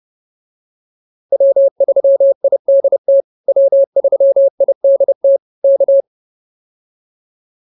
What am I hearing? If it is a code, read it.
Morse code, heard as a keyed tone, about 30 wpm, 550 Hz. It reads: W3IDT W3IDT K